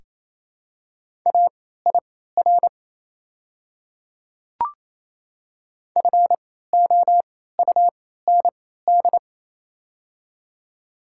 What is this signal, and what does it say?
Morse code, heard as a keyed tone, about 28 wpm, spaced as for 16 wpm, 700 Hz. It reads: AIR FOUND
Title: 3x Character Spacing - Top 300 Words - 28wpm